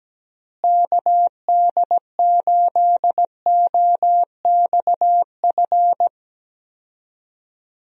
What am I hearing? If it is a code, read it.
Morse code, heard as a keyed tone, about 17 wpm, 700 Hz. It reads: KD8OXF
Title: USA Call Signs - 6 Characters - Repeated 3x - 17wpm